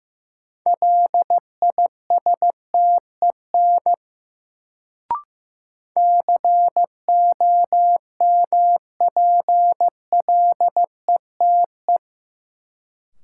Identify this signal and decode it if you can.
Morse code, heard as a keyed tone, about 15 wpm, 700 Hz. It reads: LISTEN COMPLETE
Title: Sentences from Top 400 Words - New Word Drill - 15wpm